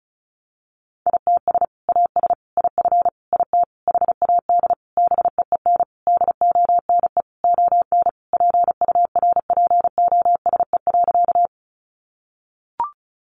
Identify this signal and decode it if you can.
Morse code, heard as a keyed tone, about 34 wpm, 700 Hz. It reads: ITS AS IF IT HAD BEEN DONE ON PURPOSE.
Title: Sentences from Top 800 Words 35wpm